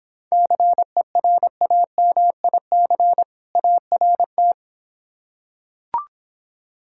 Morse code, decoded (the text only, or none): CERAMIC ART